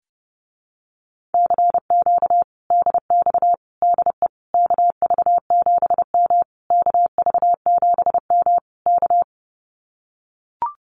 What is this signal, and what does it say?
Morse code, heard as a keyed tone, about 30 wpm, 700 Hz. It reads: CQ DX DE KV7M KV7M K
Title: Common QSO Phrases Encore 30wpm